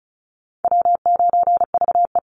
Morse code, heard as a keyed tone, about 35 wpm, 700 Hz. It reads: W9VE
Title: Call Signs - 35wpm